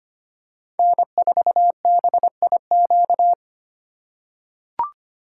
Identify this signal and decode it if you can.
Morse code, heard as a keyed tone, about 25 wpm, 700 Hz. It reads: N4BIQ
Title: USA Call Signs - 5 Characters - 2x Character Spacing - 25wpm